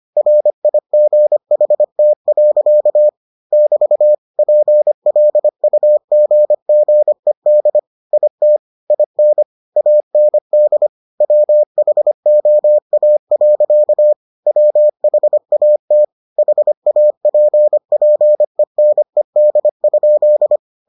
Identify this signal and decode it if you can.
Morse code, heard as a keyed tone, about 25 wpm, 600 Hz. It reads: RIGHT. <BT> PLUGGED IT IN AND WHOA. WHAT HAPPENED?